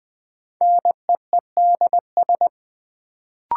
Morse code, heard as a keyed tone, about 20 wpm, 700 Hz. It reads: NEEDS
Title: Sentences from Top 1000 Words - Review Part 3 - New Word Drill - 20wpm